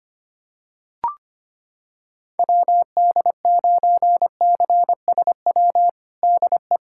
Morse code, heard as a keyed tone, about 25 wpm, 700 Hz. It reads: WD9CSW DE
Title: USA Callsign DE Callsign 25wpm